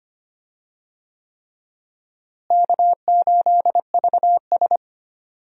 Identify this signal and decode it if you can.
Morse code, heard as a keyed tone, about 25 wpm, 700 Hz. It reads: K8VS
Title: USA Call Signs - 4 Characters - Repeated 3x - 25wpm